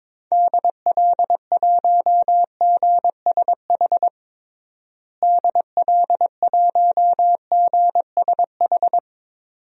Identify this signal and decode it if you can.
Morse code, heard as a keyed tone, about 21 wpm, 700 Hz. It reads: DL1GSH DL1GSH
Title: DXCC-100 Call Signs - 6 Characters - Repeated 2x - 22wpm